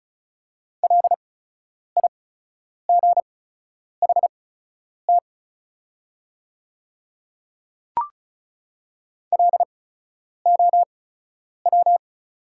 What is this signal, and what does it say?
Morse code, heard as a keyed tone, about 36 wpm, spaced as for 10 wpm, 690 Hz. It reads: LIGHT LOW
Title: 8x Character Spacing - Top 200 Words - 35wpm